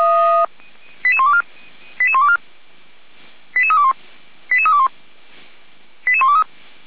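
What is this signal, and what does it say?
Five-tone selective call (ZVEI), heard as quick runs of five tones, tones of about 70 ms. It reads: 88124 88124 88321 88321 88123
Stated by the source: ZVEI five-tone selective call 88120 received by SDR.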